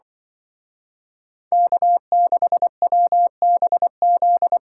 Morse code, heard as a keyed tone, about 24 wpm, 700 Hz. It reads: K6WBZ
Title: Speed Racing - USA Call Signs 20wpm